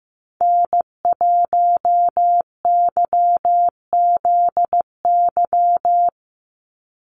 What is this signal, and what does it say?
Morse code, heard as a keyed tone, about 15 wpm, 700 Hz. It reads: N1YZY
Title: USA Call Signs 15wpm